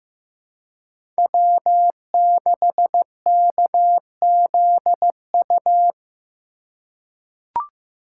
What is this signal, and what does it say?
Morse code, heard as a keyed tone, about 15 wpm, 700 Hz. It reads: W6KZU
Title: USA Call Signs - 5 Characters - 4x Character Spacing - 15wpm